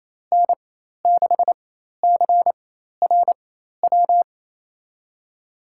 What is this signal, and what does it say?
Morse code, heard as a keyed tone, about 28 wpm, spaced as for 13 wpm, 700 Hz. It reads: N6CRW
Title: USA Call Signs - 4x Character Spacing - 28wpm